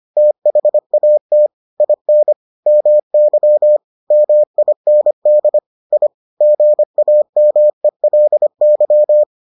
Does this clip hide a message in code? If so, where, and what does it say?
Yes: Morse code from 0.2 to 9.2 s, heard as a keyed tone, about 25 wpm, 600 Hz. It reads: THAT IN MY MIND I GAMELY